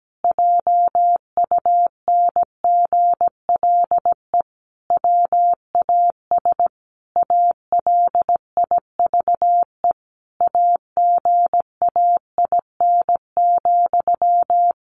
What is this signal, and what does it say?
Morse code, heard as a keyed tone, about 17 wpm, 700 Hz. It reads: JUNGLE WAS ALIVE AGAIN,
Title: A Sound Of Thunder 17wpm